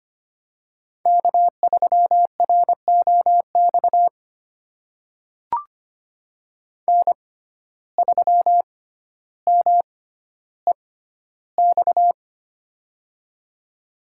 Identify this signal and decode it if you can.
Morse code, heard as a keyed tone, about 25 wpm, spaced as for 13 wpm, 700 Hz. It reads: K3ROX N3MEX
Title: USA Call Signs - 5 Characters - 6x Character Spacing - 25wpm